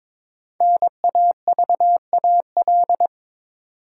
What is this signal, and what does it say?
Morse code, heard as a keyed tone, about 22 wpm, 700 Hz. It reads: NAVAL
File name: 5 Letter Words - First 3 Letters Given - 22wpm